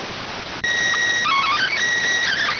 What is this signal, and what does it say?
SSTV (slow-scan television), PD 290.